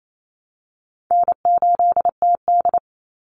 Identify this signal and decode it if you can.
Morse code, heard as a keyed tone, about 28 wpm, 700 Hz. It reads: N8TD
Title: Easy Call Signs Repeated 3x - 28wpm